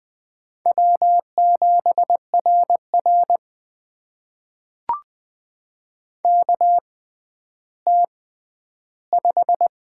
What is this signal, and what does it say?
Morse code, heard as a keyed tone, about 20 wpm, spaced as for 11 wpm, 700 Hz. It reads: W7RR KT5
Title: USA Call Signs - 4 Characters - 6x Character Spacing - 20wpm